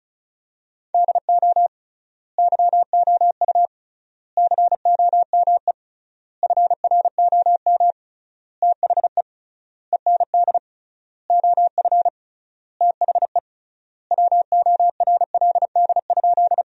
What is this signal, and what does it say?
Morse code, heard as a keyed tone, about 36 wpm, 700 Hz. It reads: DO YOU COME FROM THE END OF THE WORLD?
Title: 3x Word-Spacing - Sentences from Top 200 Words - 35wpm